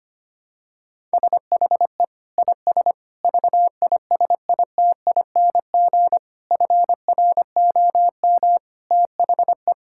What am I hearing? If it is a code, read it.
Morse code, heard as a keyed tone, about 25 wpm, 700 Hz. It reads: SHE IS VISITING FROM THE
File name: practice / 25wpm / Sentences with 4-Word Phrases - 25wpm